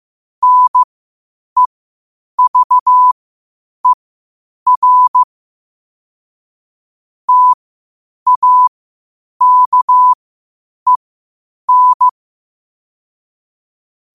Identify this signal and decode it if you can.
Morse code, heard as a keyed tone, about 15 wpm, spaced as for 8 wpm, 1000 Hz. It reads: NEVER TAKEN
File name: qotdfiles / 8wpm